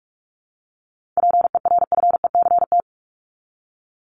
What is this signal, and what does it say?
Morse code, heard as a keyed tone, about 44 wpm, 700 Hz. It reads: PERFECT
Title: LY Suffix - Top 100 Words with Base Words 45wpm